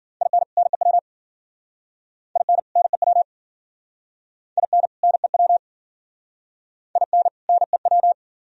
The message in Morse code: IN NEW IN NEW IN NEW IN NEW

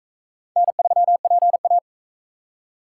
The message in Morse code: N3PA

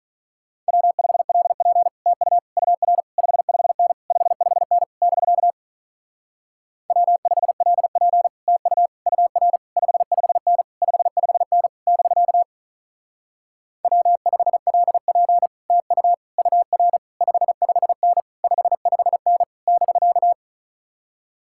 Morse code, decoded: W5LP TU UR 55N 55N <BK> W5LP TU UR 55N 55N <BK> W5LP TU UR 55N 55N <BK>

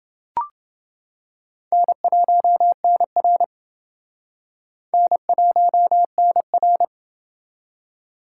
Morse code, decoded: N1NR N1NR

